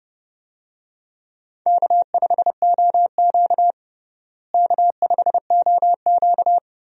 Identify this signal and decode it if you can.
Morse code, heard as a keyed tone, about 32 wpm, 700 Hz. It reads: K5OQ K5OQ
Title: USA Call Signs - 4 Characters - Repeated 2x - 30wpm